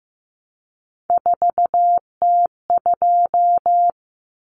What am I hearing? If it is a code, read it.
Morse code, heard as a keyed tone, about 15 wpm, 700 Hz. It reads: 4T2
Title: Common Cut Numbers - 3-Digit 15wpm